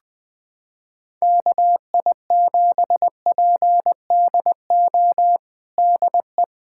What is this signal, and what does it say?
Morse code, heard as a keyed tone, about 21 wpm, 700 Hz. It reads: KI7PDO DE
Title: USA Callsign DE Callsign - 6 Characters - 20wpm